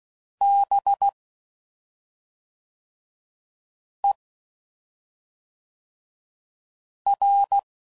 Morse code, heard as a keyed tone, about 16 wpm, 800 Hz. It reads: B E R
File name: Suffixes / Special ING's / SlowToFaster ING's